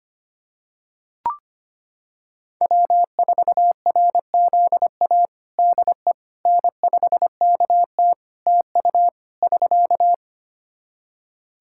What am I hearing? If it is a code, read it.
Morse code, heard as a keyed tone, about 25 wpm, 700 Hz. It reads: W4RZA DE N5KT TU <SK>
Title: Practice QSO - Exchange 4 - Ending - 25wpm